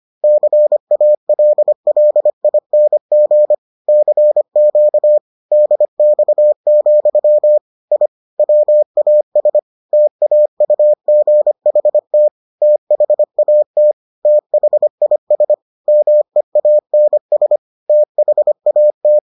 Morse code, heard as a keyed tone, about 25 wpm, 600 Hz. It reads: CALLING CQ DX, I WAS TAUGHT THAT THIS MEANS THAT